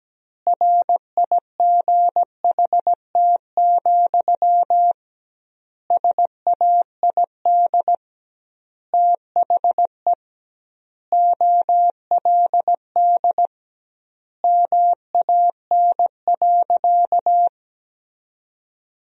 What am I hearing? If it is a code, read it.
Morse code, heard as a keyed tone, about 17 wpm, 700 Hz. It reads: RIGHT, SAID THE OLD MAN.
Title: 2x Word-Spacing - Sentences from Top 200 Words - 17wpm